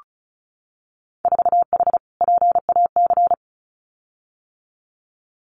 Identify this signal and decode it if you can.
Morse code, heard as a keyed tone, about 36 wpm, 700 Hz. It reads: 4H PAC
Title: Winter Field Day Exchanges Encore 35wpm